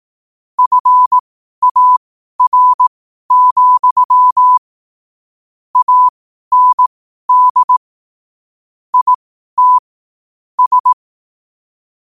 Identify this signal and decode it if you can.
Morse code, heard as a keyed tone, about 18 wpm, spaced as for 12 wpm, 1000 Hz. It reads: FAR, AND IT S
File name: qotdfiles / 12wpm